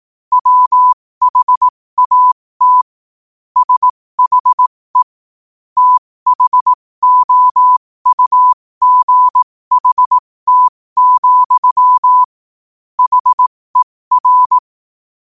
Morse code, heard as a keyed tone, about 18 wpm, spaced as for 15 wpm, 1000 Hz. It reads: WHAT SHE THOUGHT, HER